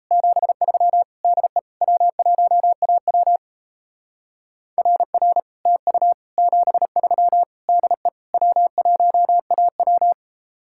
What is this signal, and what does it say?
Morse code, heard as a keyed tone, about 35 wpm, 700 Hz. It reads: 73 DE W1AW RR TU 73 DE W1AW